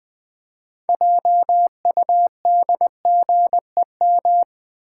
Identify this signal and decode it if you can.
Morse code, heard as a keyed tone, about 20 wpm, 700 Hz. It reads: JUDGEM
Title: Build A Word - 3 Letters At A Time - 9-Letter Words - 20wpm